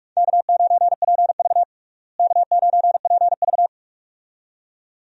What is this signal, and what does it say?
Morse code, heard as a keyed tone, about 42 wpm, 700 Hz. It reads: K9PV K9PV
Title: USA Call Signs - 4 Characters - Repeated 2x - 45wpm